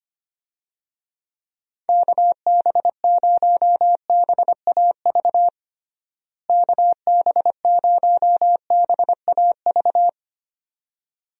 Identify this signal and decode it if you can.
Morse code, heard as a keyed tone, about 25 wpm, 700 Hz. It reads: KB0BAV KB0BAV
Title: USA Call Signs - 6 Characters - Repeated 2x - 25wpm